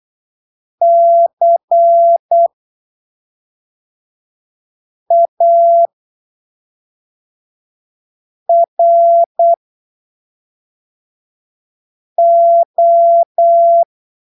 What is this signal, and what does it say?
Morse code, heard as a keyed tone, about 8 wpm, spaced as for 3 wpm, 680 Hz. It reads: CARO
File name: arquivos / texto / texto 26 08ppm